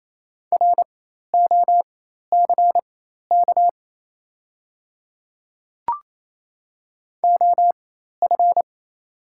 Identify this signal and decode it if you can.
Morse code, heard as a keyed tone, about 28 wpm, spaced as for 13 wpm, 690 Hz. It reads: ROCK OF
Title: 4x Character Spacing - Top 500 Words - 28wpm